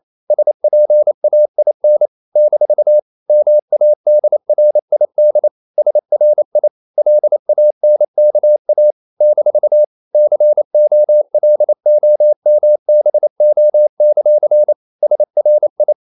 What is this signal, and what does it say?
Morse code, heard as a keyed tone, about 28 wpm, 600 Hz. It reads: SPAIN - MADRID SRI LANKA - COLOMBO; SRI
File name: CW / Practice / NationalCapitals28